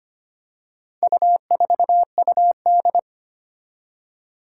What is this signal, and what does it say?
Morse code, heard as a keyed tone, about 25 wpm, 700 Hz. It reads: U4UD
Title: Four Challenging Letter-Number Combinations 25wpm